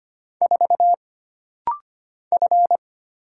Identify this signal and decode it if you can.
Morse code, heard as a keyed tone, about 25 wpm, 700 Hz. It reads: 4 F
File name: Single Character (Letter:Number:Punctuation:Prosign) - Rapid-Fire - 25wpm